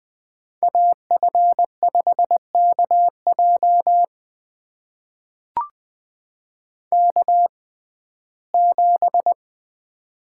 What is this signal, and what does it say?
Morse code, heard as a keyed tone, about 20 wpm, spaced as for 14 wpm, 700 Hz. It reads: AF5KJ K7
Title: USA Call Signs - 5 Characters - 6x Character Spacing - 20wpm